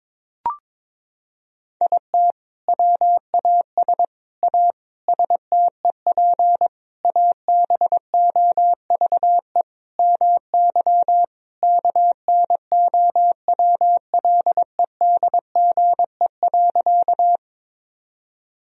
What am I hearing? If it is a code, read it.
Morse code, heard as a keyed tone, about 22 wpm, 700 Hz. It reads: IT WAS A STEP ABOVE MY KNOWLEDGE.
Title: Sentences from Top 1000 Words - Review Part 3 - 22wpm